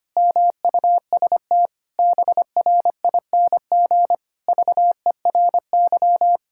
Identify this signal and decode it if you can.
Morse code, heard as a keyed tone, about 25 wpm, 700 Hz. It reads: MUST BRING VERY
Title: Pangram Challenge v2 - 25wpm